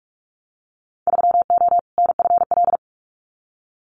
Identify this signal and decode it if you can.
Morse code, heard as a keyed tone, about 44 wpm, 700 Hz. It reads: 3O NFL
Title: Winter Field Day Exchanges 45wpm